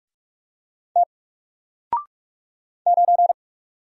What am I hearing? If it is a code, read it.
Morse code, heard as a keyed tone, about 43 wpm, 700 Hz. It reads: T 9
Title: Single Letter-Number Alt 45wpm